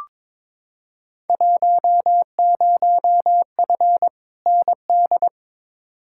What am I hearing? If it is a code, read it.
Morse code, heard as a keyed tone, about 22 wpm, 700 Hz. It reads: 10F ND
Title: Field Day Exchanges Encore - 22wpm